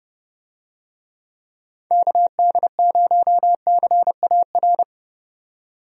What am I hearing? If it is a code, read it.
Morse code, heard as a keyed tone, about 30 wpm, 700 Hz. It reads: KD0CAR
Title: USA Call Signs - 6 Characters - 30wpm